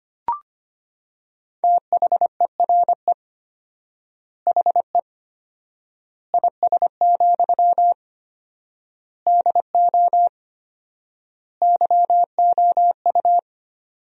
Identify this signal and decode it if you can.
Morse code, heard as a keyed tone, about 25 wpm, 700 Hz. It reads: THERE HE IS, DO YOU
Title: 4x Word-Spacing - Sentences from Top 100 Words - 25wpm